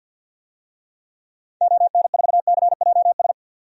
Morse code, heard as a keyed tone, about 51 wpm, 700 Hz. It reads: ON4CJS
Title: DXCC-100 Call Signs - 6 Characters - 50wpm